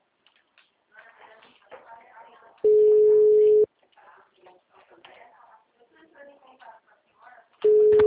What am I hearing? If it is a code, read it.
Ringback tone (Europe, 425 Hz), with speech.
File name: out- -2016- - - .82942